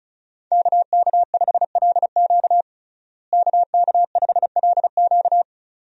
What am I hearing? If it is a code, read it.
Morse code, heard as a keyed tone, about 36 wpm, 700 Hz. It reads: KK5LQ KK5LQ
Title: USA Call Signs - 5 Characters - Repeated 2x - 35wpm